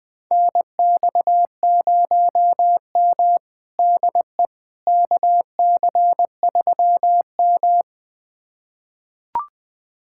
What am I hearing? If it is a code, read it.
Morse code, heard as a keyed tone, about 20 wpm, 700 Hz. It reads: NX0M DE KC3M
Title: USA Callsign DE Callsign - 4 Characters - 20wpm